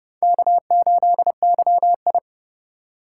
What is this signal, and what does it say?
Morse code, heard as a keyed tone, about 30 wpm, 700 Hz. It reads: K8YI
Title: USA Call Signs - 4x Character Spacing - 30wpm